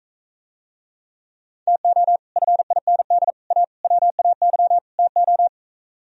Morse code, heard as a keyed tone, about 45 wpm, 700 Hz. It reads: TO FIND A WAY TO